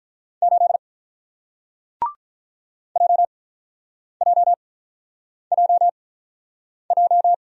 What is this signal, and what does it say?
Morse code, heard as a keyed tone, about 48 wpm, spaced as for 9 wpm, 700 Hz. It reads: 8 JJJJ